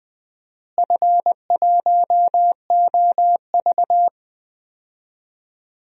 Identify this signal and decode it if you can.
Morse code, heard as a keyed tone, about 19 wpm, 700 Hz. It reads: F1OV
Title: DXCC-100 Call Signs - 4 Characters - Repeated 3x - 20wpm